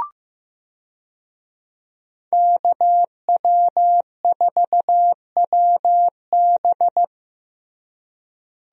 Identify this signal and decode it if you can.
Morse code, heard as a keyed tone, about 15 wpm, 700 Hz. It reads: KW4WB
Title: USA Call Signs - 5 Characters - Repeated 2x - 15wpm